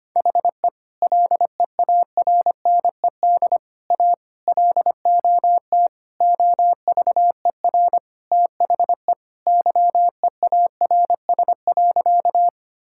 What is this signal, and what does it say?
Morse code, heard as a keyed tone, about 25 wpm, 700 Hz. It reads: HE LEARNED A LOT OVER THE YEARS.
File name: practice / 25wpm / Sentences with 3-Word Phrases - 25wpm